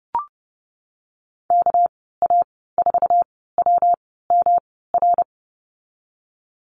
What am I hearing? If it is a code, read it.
Morse code, heard as a keyed tone, about 32 wpm, spaced as for 17 wpm, 700 Hz. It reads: KA4WMR
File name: Lesson 37g - Letter x - Call Signs 30wpm